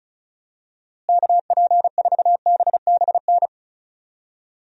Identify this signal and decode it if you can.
Morse code, heard as a keyed tone, about 36 wpm, 700 Hz. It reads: KP4BBN
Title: DXCC-100 Call Signs - 6 Characters - Repeated 2x - 35wpm